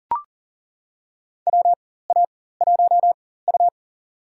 Morse code, heard as a keyed tone, about 37 wpm, spaced as for 18 wpm, 700 Hz. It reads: WA1U
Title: USA Call Signs - 4 Characters - 4x Character Spacing - 40wpm